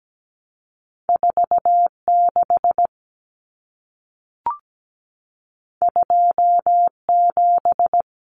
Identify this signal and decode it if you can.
Morse code, heard as a keyed tone, about 17 wpm, 700 Hz. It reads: 46 27
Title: Numbers - 2-Digit 17wpm